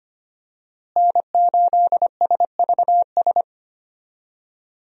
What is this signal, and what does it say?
Morse code, heard as a keyed tone, about 25 wpm, 700 Hz. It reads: N8SVS
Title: USA Call Signs - 5 Characters - Repeated 2x - 25wpm